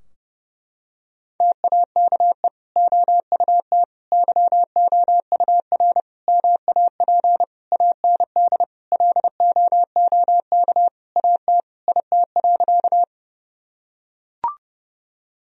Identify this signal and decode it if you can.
Morse code, heard as a keyed tone, about 32 wpm, 700 Hz. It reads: TAKE OUT YOUR MAP AND LOOK AT IT.
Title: Sentences from Top 1000 Words - Review Part 2 - 30wpm